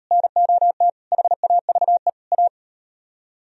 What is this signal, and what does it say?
Morse code, heard as a keyed tone, about 38 wpm, 700 Hz. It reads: NOT HAVE A